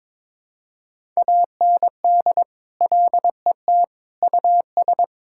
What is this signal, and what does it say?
Morse code, heard as a keyed tone, about 22 wpm, 700 Hz. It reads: AND LET US
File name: practice / 22wpm / Sets of 7 Words - 22wpm